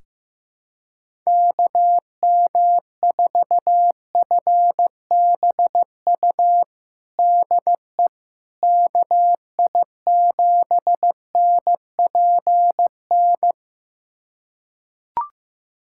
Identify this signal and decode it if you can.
Morse code, heard as a keyed tone, about 15 wpm, 700 Hz. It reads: KM4FBU DE KI7NPN
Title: USA Callsign DE Callsign 15wpm